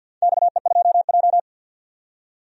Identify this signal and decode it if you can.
Morse code, heard as a keyed tone, about 47 wpm, 700 Hz. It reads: XE2J
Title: DXCC-100 Call Signs - 4 Characters - 50wpm